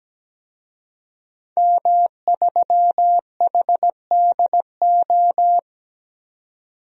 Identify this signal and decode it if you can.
Morse code, heard as a keyed tone, about 17 wpm, 700 Hz. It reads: M3HDO
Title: DXCC-100 Call Signs - 5 Characters - 17wpm